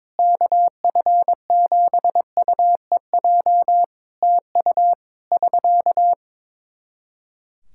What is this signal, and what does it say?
Morse code, heard as a keyed tone, about 22 wpm, 700 Hz. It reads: KF7UEJ TU <SK>
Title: K1USN SST QSO Practice 22wpm